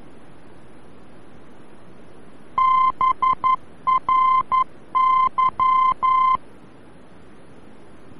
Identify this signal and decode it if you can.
Morse code, heard as a keyed tone, about 11 wpm, 1040 Hz. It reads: BRY
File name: BRY-VOR-F